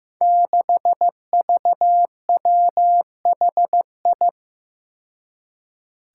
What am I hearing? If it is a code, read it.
Morse code, heard as a keyed tone, about 15 wpm, 700 Hz. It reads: 6VWHI
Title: Five Challenging Letter-Number Combinations 15wpm